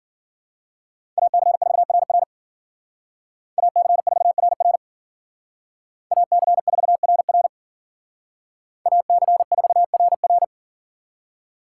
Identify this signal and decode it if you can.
Morse code, heard as a keyed tone, about 51 wpm, 700 Hz. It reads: AC4RR AC4RR AC4RR AC4RR